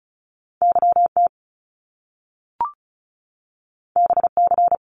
Morse code, heard as a keyed tone, about 36 wpm, 700 Hz. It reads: YT BC
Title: Canadian Provinces and Territories 35wpm